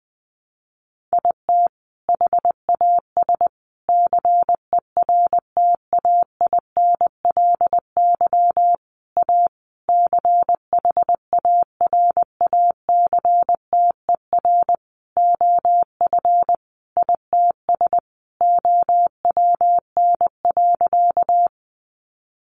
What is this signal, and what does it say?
Morse code, heard as a keyed tone, about 20 wpm, 700 Hz. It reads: IT HAS CERTAINLY A CHARACTER OF ITS OWN.
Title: Sentences from Top 1200 Words 20wpm